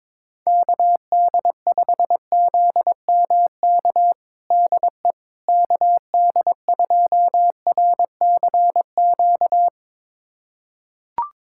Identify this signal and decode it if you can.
Morse code, heard as a keyed tone, about 22 wpm, 700 Hz. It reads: KD5ZMK DE KD2RCQ
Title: USA Callsign DE Callsign - 6 Characters - 22wpm